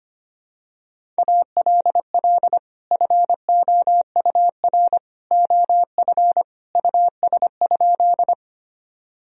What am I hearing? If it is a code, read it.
Morse code, heard as a keyed tone, about 25 wpm, 700 Hz. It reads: ALL FOUR OF US?
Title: Sentences from Top 1000 Words - Review Part 1 - 25wpm